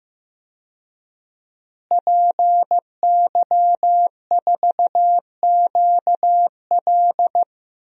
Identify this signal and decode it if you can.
Morse code, heard as a keyed tone, about 15 wpm, 700 Hz. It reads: PY4QL
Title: DXCC-100 Call Signs - 5 Characters - Repeated 2x - 15wpm